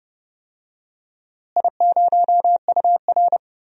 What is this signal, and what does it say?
Morse code, heard as a keyed tone, about 30 wpm, 700 Hz. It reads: I0UR
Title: DXCC-100 Call Signs - 4 Characters - Repeated 2x - 30wpm